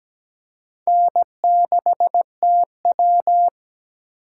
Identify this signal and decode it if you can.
Morse code, heard as a keyed tone, about 17 wpm, 700 Hz. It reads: N6TW
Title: K1USN SST QSO Practice - 17wpm